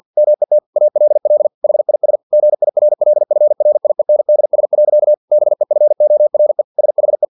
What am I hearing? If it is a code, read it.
Morse code, heard as a keyed tone, about 50 wpm, 600 Hz. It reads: MET ALL HIS GIRLFRIENDS. BEFORE SHE
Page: Never 49wpm